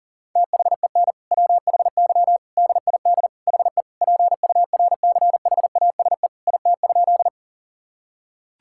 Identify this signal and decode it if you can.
Morse code, heard as a keyed tone, about 40 wpm, 700 Hz. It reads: THEN WHY DID HE PURCHASE IT?
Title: Sentences from Top 1000 Words - Review Part 5 - 40wpm